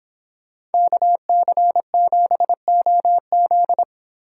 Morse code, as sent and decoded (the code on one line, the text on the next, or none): -.- -.-. --... --- --..
KC7OZ